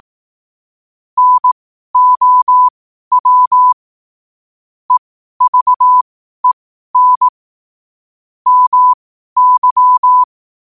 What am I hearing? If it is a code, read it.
Morse code, heard as a keyed tone, about 18 wpm, spaced as for 12 wpm, 1000 Hz. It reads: NOW EVEN MY
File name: qotdfiles / 12wpm